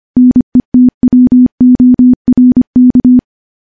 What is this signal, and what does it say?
Morse code, heard as a keyed tone, about 25 wpm, 250 Hz. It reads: NETWORK
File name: morse / morse-training-25WPM